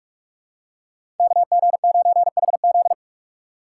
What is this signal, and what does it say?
Morse code, heard as a keyed tone, about 44 wpm, 700 Hz. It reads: KG9HZ
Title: USA Call Signs - 5 Characters - 3x Character Spacing - 45wpm